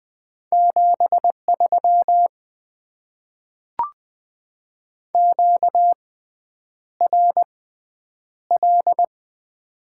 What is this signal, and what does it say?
Morse code, heard as a keyed tone, about 20 wpm, spaced as for 9 wpm, 700 Hz. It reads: 73 QRL
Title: Top 20 QSO Elements - 6x Character Spacing - 20wpm